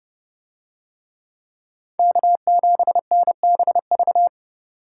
Morse code, heard as a keyed tone, about 28 wpm, 700 Hz. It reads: K7NBV